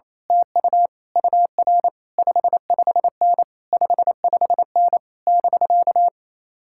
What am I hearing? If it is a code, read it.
Morse code, heard as a keyed tone, about 27 wpm, 700 Hz. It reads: TU UR 55N 55N <BK>
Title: Speed-Racing Common QSO Phrases - 28wpm